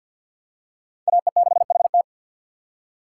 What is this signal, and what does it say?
Morse code, heard as a keyed tone, about 52 wpm, 700 Hz. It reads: AE6HT
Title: USA Call Signs - 5 Characters - 3x Character Spacing - 50wpm